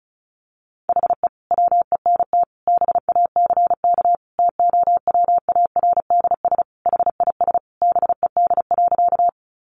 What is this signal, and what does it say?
Morse code, heard as a keyed tone, about 34 wpm, 700 Hz. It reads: HE WENT BACK TOWARDS HIS BED.
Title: Sentences from Top 1100 Words 35wpm